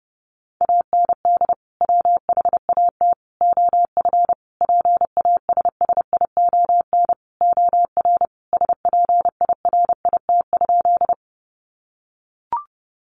Morse code, heard as a keyed tone, about 30 wpm, 700 Hz. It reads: AND WHAT OF PASSION OR SPIRIT?
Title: Sentences from Top 1400 Words 30wpm